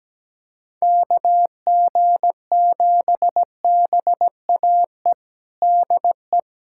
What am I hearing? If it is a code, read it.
Morse code, heard as a keyed tone, about 17 wpm, 700 Hz. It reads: KG7BAE DE
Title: USA Callsign DE Callsign - 6 Characters - 17wpm